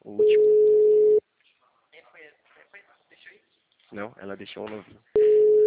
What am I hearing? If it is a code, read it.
Ringback tone (Europe, 425 Hz), with speech.
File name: out- -2038- - - .10933